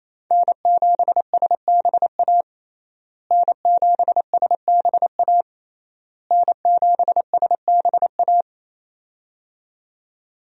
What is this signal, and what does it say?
Morse code, heard as a keyed tone, about 27 wpm, 700 Hz. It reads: N7SBA N7SBA N7SBA
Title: USA Call Signs - 5 Characters - Repeated 3x - 28wpm